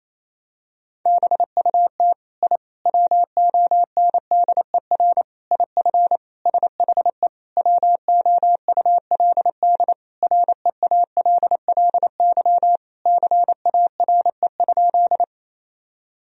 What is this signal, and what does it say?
Morse code, heard as a keyed tone, about 28 wpm, 700 Hz. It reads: BUT I WONDER IF SHE WOULD REALLY CARE?
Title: Sentences from Top 1000 Words - Review Part 3 - 28wpm